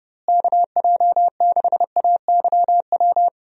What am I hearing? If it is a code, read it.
Morse code, heard as a keyed tone, about 30 wpm, 700 Hz. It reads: KJ6AYW
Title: USA Call Signs - 6 Characters - Repeated 3x - 30wpm